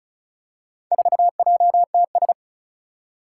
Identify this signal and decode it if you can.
Morse code, heard as a keyed tone, about 36 wpm, 700 Hz. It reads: 4JTS
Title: Four Challenging Letter-Number Combinations 35wpm